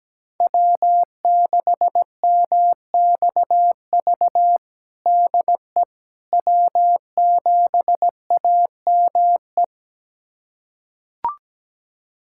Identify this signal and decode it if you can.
Morse code, heard as a keyed tone, about 17 wpm, 700 Hz. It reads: W6MXV DE W7AME
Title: USA Callsign DE Callsign - 5 Characters - 17wpm